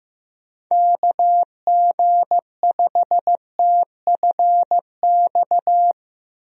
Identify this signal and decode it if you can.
Morse code, heard as a keyed tone, about 15 wpm, 700 Hz. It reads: KG5TFX
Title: USA Call Signs - 6 Characters - Repeated 2x - 15wpm